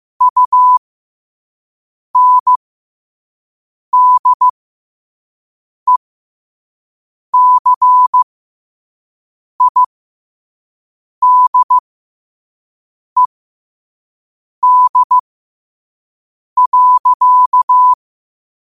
Morse code, heard as a keyed tone, about 15 wpm, spaced as for 5 wpm, 1000 Hz. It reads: UNDECIDED.
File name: 5wpm